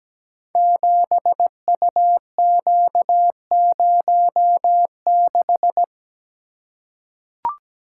Morse code, heard as a keyed tone, about 17 wpm, 700 Hz. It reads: 7UQ06
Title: Code Groups - 5 Letter:Numbers - 17wpm